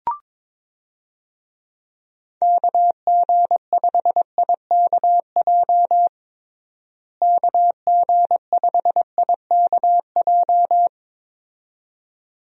Morse code, heard as a keyed tone, about 22 wpm, 700 Hz. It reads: KG5IKJ KG5IKJ
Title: USA Call Signs - 6 Characters - Repeated 2x - 22wpm